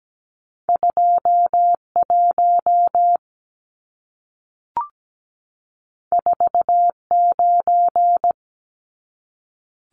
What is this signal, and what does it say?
Morse code, heard as a keyed tone, about 17 wpm, 700 Hz. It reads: 21 49
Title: Numbers - 2-Digit 17wpm